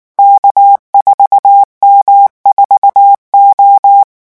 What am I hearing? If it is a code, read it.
Morse code, heard as a keyed tone, about 19 wpm, 800 Hz. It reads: K4M4O
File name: morse mp3 / revision15-vitesse 18 mots